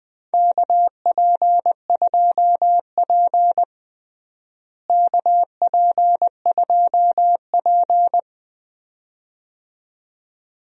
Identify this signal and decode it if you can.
Morse code, heard as a keyed tone, about 20 wpm, 700 Hz. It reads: KP2P KP2P
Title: DXCC-100 Call Signs - 4 Characters - Repeated 2x - 20wpm